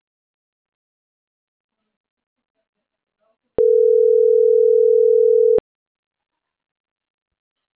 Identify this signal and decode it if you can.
Ringback tone (North America), 440 + 480 Hz.